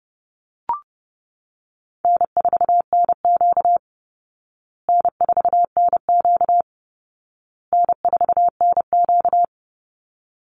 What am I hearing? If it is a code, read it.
Morse code, heard as a keyed tone, about 30 wpm, 700 Hz. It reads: N4NQ N4NQ N4NQ
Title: Easy Call Signs Repeated 3x - 30wpm